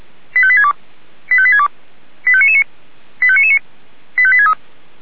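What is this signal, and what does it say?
Five-tone selective call (ZVEI), heard as quick runs of five tones, tones of about 70 ms. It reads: 85862 85862 85999 85999 85863
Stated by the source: Five-tone ZVEI selective call received on an SDR.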